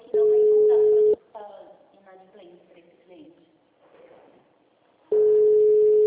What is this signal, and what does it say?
Ringback tone (Europe, 425 Hz), with speech.